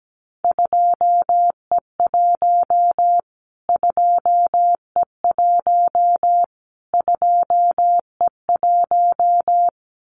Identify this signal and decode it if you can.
Morse code, heard as a keyed tone, about 17 wpm, 700 Hz. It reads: 2E1 2E1 2E1
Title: Callsign Prefixes for Top 25 DXCC - Repeated 3x - 17wpm